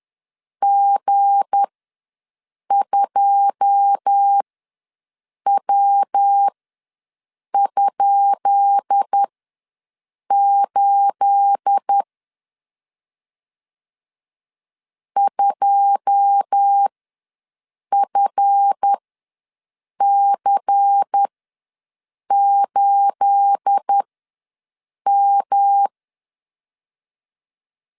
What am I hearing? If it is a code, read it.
Morse code, heard as a keyed tone, about 10 wpm, spaced as for 6 wpm, 800 Hz. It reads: G2W?8 2FC8M
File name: LEZIONI / 10 SIGNS / 10B INT ws 10wpm 4134